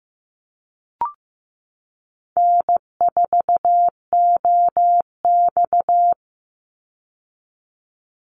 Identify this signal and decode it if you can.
Morse code, heard as a keyed tone, about 15 wpm, 700 Hz. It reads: N4OX
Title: Easy Call Signs - 15wpm